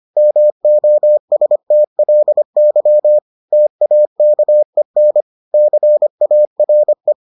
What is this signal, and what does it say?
Morse code, heard as a keyed tone, about 25 wpm, 600 Hz. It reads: MOSTLY TAKEN CARE